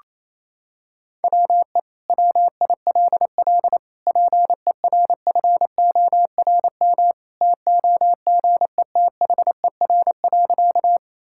Morse code, heard as a keyed tone, about 28 wpm, 700 Hz. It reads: WE WILL PERFORM TOGETHER.
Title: Sentences from Top 1000 Words - Review Part 5 - 28wpm